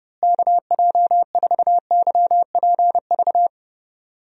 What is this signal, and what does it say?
Morse code, heard as a keyed tone, about 30 wpm, 700 Hz. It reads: KJ4YPV
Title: USA Call Signs - 6 Characters - Repeated 3x - 30wpm